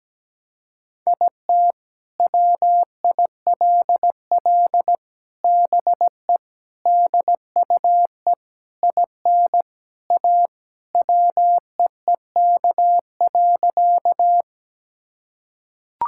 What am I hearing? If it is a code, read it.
Morse code, heard as a keyed tone, about 17 wpm, 700 Hz. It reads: IT WILL BE DUE IN A WEEK.
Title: Sentences from Top 1000 Words - Review Part 6 - 17wpm